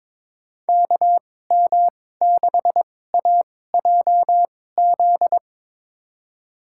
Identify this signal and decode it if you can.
Morse code, heard as a keyed tone, about 23 wpm, spaced as for 16 wpm, 700 Hz. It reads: KM6AJZ
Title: USA Call Signs - 6 Characters - 2x Character Spacing - 22wpm